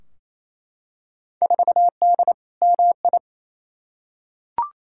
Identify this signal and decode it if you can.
Morse code, heard as a keyed tone, about 28 wpm, 700 Hz. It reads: 4D MI
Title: Field Day Exchanges Encore - 28wpm